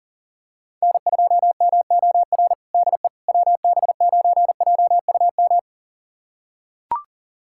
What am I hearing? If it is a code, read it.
Morse code, heard as a keyed tone, about 37 wpm, 700 Hz. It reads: N2MOR DE WB9JUM
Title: USA Callsign DE Callsign 40wpm